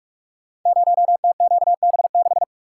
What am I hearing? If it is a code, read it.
Morse code, heard as a keyed tone, about 44 wpm, 690 Hz. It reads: 0TQB6
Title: Code Groups - 5 Letter:Numbers - 45wpm